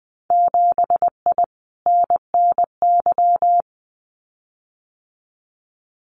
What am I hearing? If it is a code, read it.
Morse code, heard as a keyed tone, about 20 wpm, 700 Hz. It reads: 7I NNY
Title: Winter Field Day Exchanges Encore 20wpm